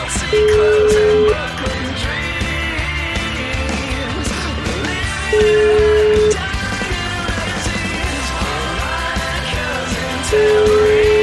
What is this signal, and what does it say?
Ringback tone (Europe, 425 Hz), with noise.